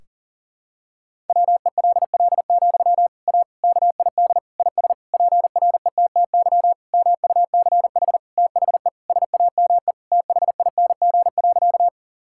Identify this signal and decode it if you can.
Morse code, heard as a keyed tone, about 40 wpm, 700 Hz. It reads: WELL, A KID IS PRETTY MUCH THE SAME THING.
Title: Sentences from Top 1600 Words 40wpm